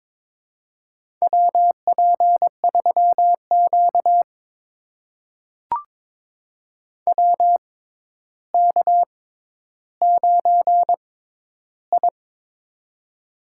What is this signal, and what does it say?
Morse code, heard as a keyed tone, about 22 wpm, spaced as for 11 wpm, 700 Hz. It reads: WP3Q WK9I
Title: USA Call Signs - 4 Characters - 6x Character Spacing - 22wpm